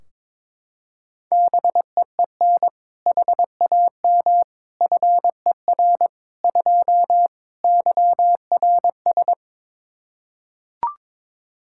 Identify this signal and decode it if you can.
Morse code, heard as a keyed tone, about 21 wpm, 700 Hz. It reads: BEEN HAM FER 2 YRS
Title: Practice QSO - Exchange 3 - More Info - 22wpm